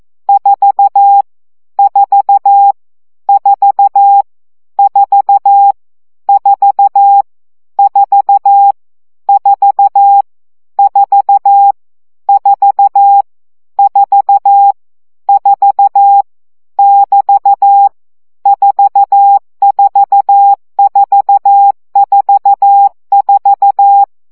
Morse code, heard as a keyed tone, about 14 wpm, 800 Hz. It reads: 4 4 4 4 4 4 4 4 4 4 4 <BT> 44444
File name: CW gyakorlas 4 12 baud